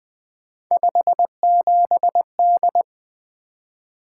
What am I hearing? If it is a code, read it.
Morse code, heard as a keyed tone, about 20 wpm, 700 Hz. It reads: 57D
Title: Three Letter-Number Combinations - 20wpm